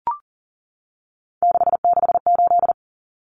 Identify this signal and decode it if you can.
Morse code, heard as a keyed tone, about 40 wpm, 700 Hz. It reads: 668
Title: Common Cut Numbers - 3-Digit 40wpm